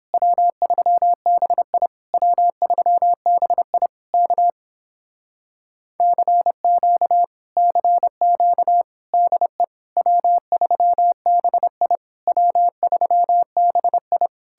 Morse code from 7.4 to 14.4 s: CQ DE W3BI W3BI

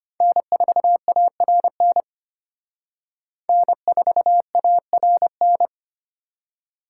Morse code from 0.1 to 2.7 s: N4ARN

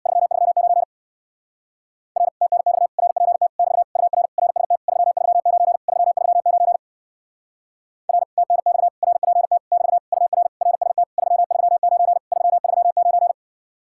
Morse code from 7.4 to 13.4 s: R TNX RPT <BT> UR RST 339 339

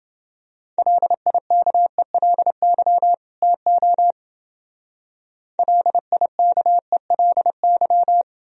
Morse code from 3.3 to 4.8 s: TO

